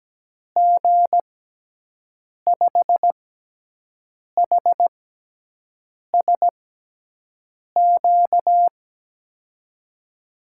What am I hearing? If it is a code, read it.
Morse code, heard as a keyed tone, about 17 wpm, spaced as for 6 wpm, 700 Hz. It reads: G5HSQ
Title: USA Call Signs - 6 Characters - 6x Character Spacing - 17wpm